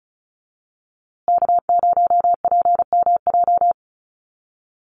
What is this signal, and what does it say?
Morse code, heard as a keyed tone, about 34 wpm, 700 Hz. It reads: K0PMJ
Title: USA Call Signs Repeated 3x 35wpm